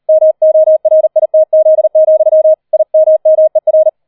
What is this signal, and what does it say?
Morse code, heard as a keyed tone, about 38 wpm, 630 Hz. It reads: MORITZ, IMMER
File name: 40wpm